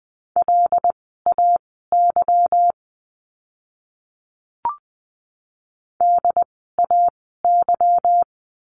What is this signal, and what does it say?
Morse code, heard as a keyed tone, about 20 wpm, spaced as for 14 wpm, 700 Hz. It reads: LAY DAY
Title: 146 Lesson 24f - Letter y - Words 20wpm